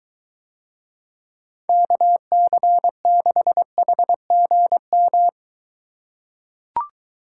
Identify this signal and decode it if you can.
Morse code, heard as a keyed tone, about 23 wpm, 700 Hz. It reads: KC6HGM